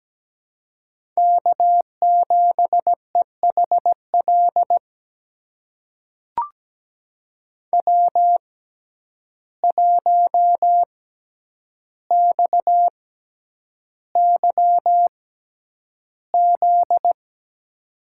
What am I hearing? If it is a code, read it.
Morse code, heard as a keyed tone, about 17 wpm, spaced as for 9 wpm, 700 Hz. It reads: K7EHL W1XYZ
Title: USA Call Signs - 5 Characters - 6x Character Spacing - 17wpm